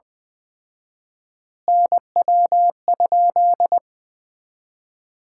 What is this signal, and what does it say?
Morse code, heard as a keyed tone, about 21 wpm, 700 Hz. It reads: NW?
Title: Common QSO Phrases - 4x Word-Spacing - 20wpm